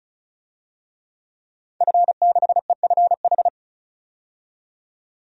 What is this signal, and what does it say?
Morse code, heard as a keyed tone, about 34 wpm, 700 Hz. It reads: F6EFH
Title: DXCC-100 Call Signs - 5 Characters - 35wpm